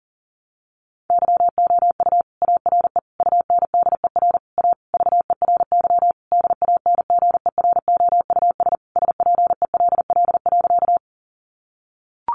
Morse code, heard as a keyed tone, about 40 wpm, 700 Hz. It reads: YOU ARE UNDER A VERY DANGEROUS SPELL.
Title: Sentences from Top 1200 Words 40wpm